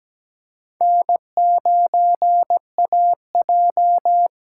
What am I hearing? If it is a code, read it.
Morse code, heard as a keyed tone, about 17 wpm, 700 Hz. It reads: N9AJ
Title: USA Call Signs - 5 Characters - 8x Character Spacing - 17wpm